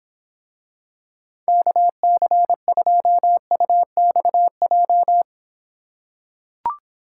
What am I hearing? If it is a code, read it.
Morse code, heard as a keyed tone, about 26 wpm, 700 Hz. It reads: KC2UXJ